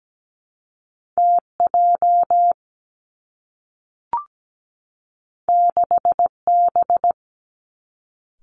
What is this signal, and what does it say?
Morse code, heard as a keyed tone, about 17 wpm, 700 Hz. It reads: TJ 6B
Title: Two Challenging Letter-Number Combinations 17wpm